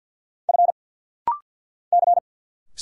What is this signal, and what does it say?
Morse code, heard as a keyed tone, about 50 wpm, 700 Hz. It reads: F C
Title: Single Letter-Number Alt - Rapid-Fire 50wpm